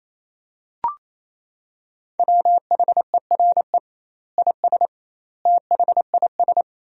Morse code, heard as a keyed tone, about 28 wpm, 700 Hz. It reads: WHERE IS THIS
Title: 2x Word-Spacing - Sentences from Top 300 Words - 28wpm